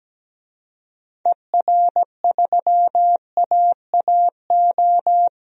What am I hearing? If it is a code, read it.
Morse code, heard as a keyed tone, about 17 wpm, 700 Hz. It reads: ER3AAO
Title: DXCC-100 Call Signs - 6 Characters - 17wpm